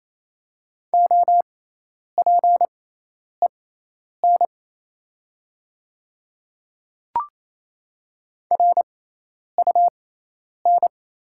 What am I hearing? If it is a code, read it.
Morse code, heard as a keyed tone, about 27 wpm, spaced as for 10 wpm, 690 Hz. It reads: OPEN RUN
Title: 6x Character Spacing - Top 300 Words - 28wpm